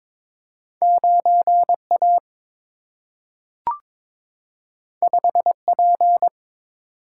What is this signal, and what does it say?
Morse code, heard as a keyed tone, about 22 wpm, 700 Hz. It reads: 9 A 5 P
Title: Single-Letter Single-Number 22wpm